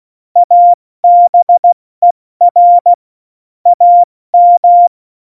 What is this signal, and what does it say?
Morse code, heard as a keyed tone, about 16 wpm, 700 Hz. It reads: ABER AM
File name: Morsekurs / Witze / Witz 17. November 2020, 70 F80